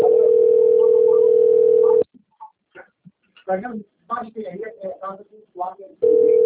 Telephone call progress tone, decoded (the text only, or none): ringback tone